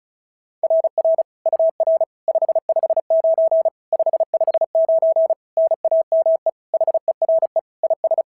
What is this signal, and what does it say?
Morse code, heard as a keyed tone, about 33 wpm, 650 Hz. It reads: RR UR 559 559 NAME HERE IS
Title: SOTA QSO Practice 35wpm